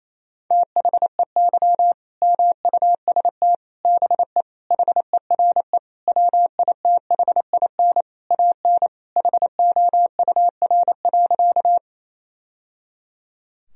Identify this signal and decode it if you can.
Morse code, heard as a keyed tone, about 28 wpm, 700 Hz. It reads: THEY MUST BE HERE WITHIN AN HOUR.
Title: Sentences from Top 1000 Words - Review Part 4 - 28wpm